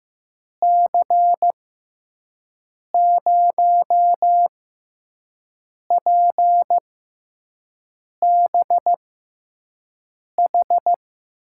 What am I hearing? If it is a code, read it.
Morse code, heard as a keyed tone, about 15 wpm, spaced as for 5 wpm, 700 Hz. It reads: C0PBH
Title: USA Call Signs - 6 Characters - 6x Character Spacing - 15wpm